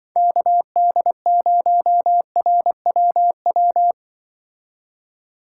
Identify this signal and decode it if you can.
Morse code, heard as a keyed tone, about 24 wpm, 700 Hz. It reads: KD0RWW
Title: Speed Racing - USA 2x3 Call Signs 20wpm